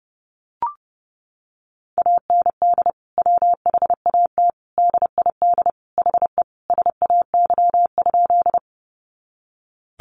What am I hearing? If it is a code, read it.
Morse code, heard as a keyed tone, about 30 wpm, 700 Hz. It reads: AND WHAT DID HE SAY?
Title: Sentences from Top 200 Words 30wpm